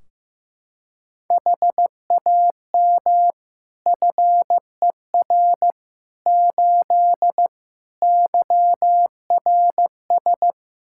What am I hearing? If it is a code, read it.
Morse code, heard as a keyed tone, about 15 wpm, 700 Hz. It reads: HAM FER 8 YRS
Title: Practice QSO - Exchange 3 - More Info - 15wpm